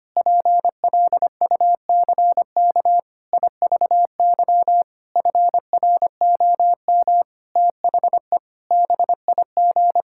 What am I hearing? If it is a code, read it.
Morse code, heard as a keyed tone, about 25 wpm, 700 Hz. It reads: PLUCK IVY FROM THE BIG
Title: Pangram Challenge v2 - 25wpm